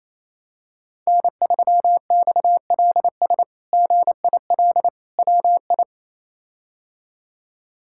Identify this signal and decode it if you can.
Morse code, heard as a keyed tone, about 28 wpm, 700 Hz. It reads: N3XLS GIL WI
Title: K1USN SST QSO Practice - 28wpm